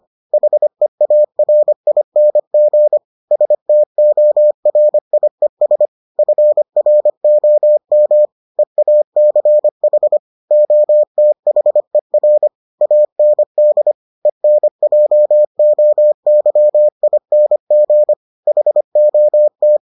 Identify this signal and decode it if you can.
Morse code, heard as a keyed tone, about 25 wpm, 600 Hz. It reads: HEARING STORIES FROM EACH OTHER AND ENJOYING HOT